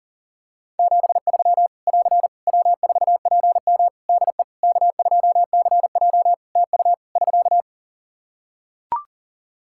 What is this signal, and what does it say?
Morse code, heard as a keyed tone, about 40 wpm, 700 Hz. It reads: 73 <AR> W4PM DE K2CJ TU <SK>
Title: K1USN SST QSO Practice 40wpm